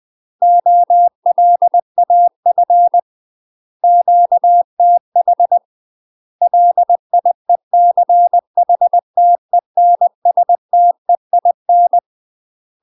Morse code, heard as a keyed tone, about 20 wpm, 700 Hz. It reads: OLAF QTH LIECHTENSTEIN
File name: UK MASTER QSOs TRAINING / 20WPM / UK MASTER QSOs 0029